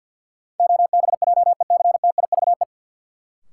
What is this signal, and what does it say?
Morse code, heard as a keyed tone, about 49 wpm, 700 Hz. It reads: OBJECTIVE
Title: Sentences from Top 1200 Words - New Word Drill - 50wpm